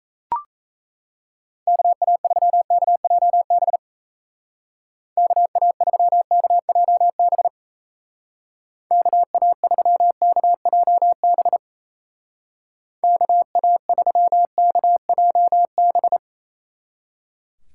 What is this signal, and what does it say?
Morse code, heard as a keyed tone, about 35 wpm, 700 Hz. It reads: KA3KJB KA3KJB KA3KJB KA3KJB